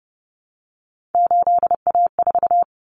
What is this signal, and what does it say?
Morse code, heard as a keyed tone, about 30 wpm, 700 Hz. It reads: 8A4
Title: Common Cut Numbers - 3-Digit 30wpm